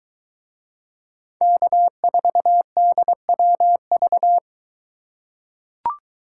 Morse code, heard as a keyed tone, about 22 wpm, 700 Hz. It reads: K4DWV